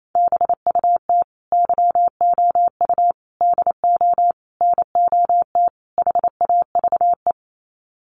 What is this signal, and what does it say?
Morse code, heard as a keyed tone, about 28 wpm, 700 Hz. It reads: BUT YOU DO NOT HAVE
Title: Sets of 5 Words Encore 28wpm